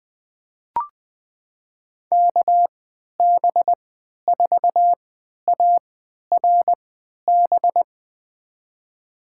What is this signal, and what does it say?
Morse code, heard as a keyed tone, about 21 wpm, spaced as for 12 wpm, 700 Hz. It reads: KB4ARB
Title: USA Call Signs - 6 Characters - 3x Character Spacing - 20wpm